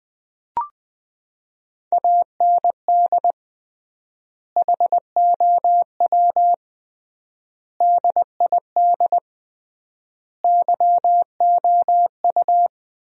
Morse code, heard as a keyed tone, about 20 wpm, 700 Hz. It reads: AND HOW DID YOU
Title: 3x Word-Spacing - Sentences from Top 100 Words - 20wpm